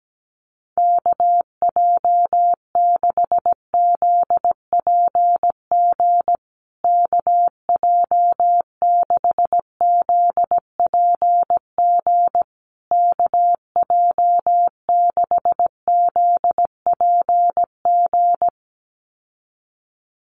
Morse code, heard as a keyed tone, about 17 wpm, 700 Hz. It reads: KJ6ZPG KJ6ZPG KJ6ZPG
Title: USA Call Signs Repeated 3x 17wpm